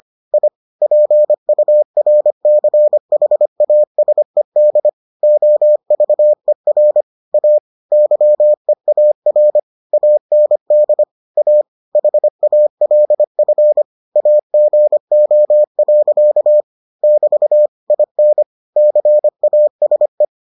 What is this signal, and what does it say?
Morse code, heard as a keyed tone, about 25 wpm, 600 Hz. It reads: I PURCHASED OVER A YEAR AND A HALF AGO. <BT> IN CASE